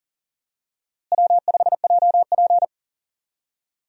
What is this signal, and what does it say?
Morse code, heard as a keyed tone, about 40 wpm, 700 Hz. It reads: W5JP
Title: USA Call Signs - 4 Characters - Repeated 3x - 40wpm